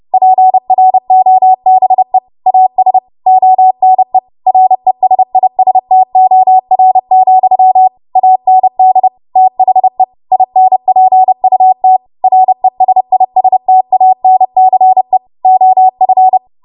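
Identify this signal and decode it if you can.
Morse code, heard as a keyed tone, about 30 wpm, 750 Hz. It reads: PROBE AS ONE RESISTOR, AND THE INPUT RESISTANCE OF
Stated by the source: W1AW Web Code Practice Files, 30 WPM run from ARRL; text from April 2023 QST.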